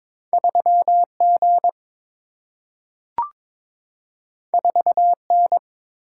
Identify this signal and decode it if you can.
Morse code, heard as a keyed tone, about 22 wpm, 700 Hz. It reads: 3G 4N
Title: Single-Letter Single-Number 22wpm